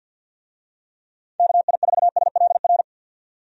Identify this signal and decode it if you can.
Morse code, heard as a keyed tone, about 52 wpm, 700 Hz. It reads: KI4SLR
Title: USA Call Signs - 6 Characters - 50wpm